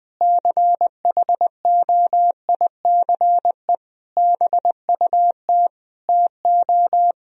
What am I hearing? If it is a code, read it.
Morse code, heard as a keyed tone, about 20 wpm, 700 Hz. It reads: CHOICE BUT TO
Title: Speed-Racing - Sets of 5 Words 17wpm